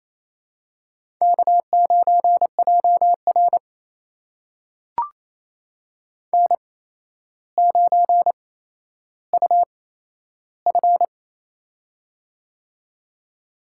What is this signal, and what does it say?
Morse code, heard as a keyed tone, about 28 wpm, spaced as for 12 wpm, 700 Hz. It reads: K9JR N9UF
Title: USA Call Signs - 4 Characters - 8x Character Spacing - 28wpm